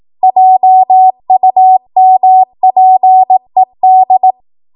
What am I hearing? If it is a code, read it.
Morse code, heard as a keyed tone, about 18 wpm, 750 Hz. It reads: JUMPED
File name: Archive / 18 WPM / 18WPM